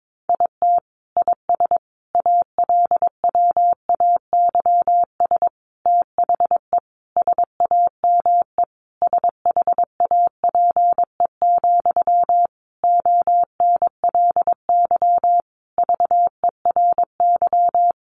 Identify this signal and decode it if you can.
Morse code, heard as a keyed tone, about 22 wpm, 700 Hz. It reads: IT IS ALWAYS THE SAME SHAPE, ONLY VERY
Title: Sentences from Top 1400 Words 22wpm